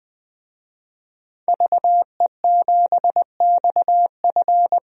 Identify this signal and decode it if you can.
Morse code, heard as a keyed tone, about 20 wpm, 700 Hz. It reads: VE7XF
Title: DXCC-100 Call Signs - 5 Characters - Repeated 2x - 20wpm